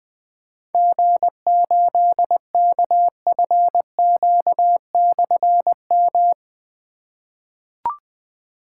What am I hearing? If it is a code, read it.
Morse code, heard as a keyed tone, about 20 wpm, 700 Hz. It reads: G8KFQ/M
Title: Difficult Call Signs 20wpm